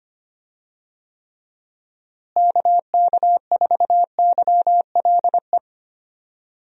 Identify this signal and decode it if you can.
Morse code, heard as a keyed tone, about 25 wpm, 700 Hz. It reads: KK4YLE
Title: USA Call Signs - 6 Characters - Repeated 2x - 25wpm